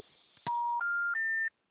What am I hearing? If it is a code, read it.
Special information tone.
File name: out- -2020- - - .6137